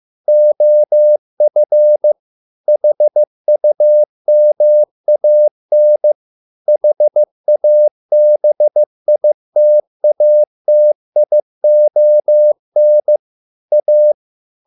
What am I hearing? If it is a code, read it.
Morse code, heard as a keyed tone, about 15 wpm, 600 Hz. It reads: OF HUMAN HABITATION A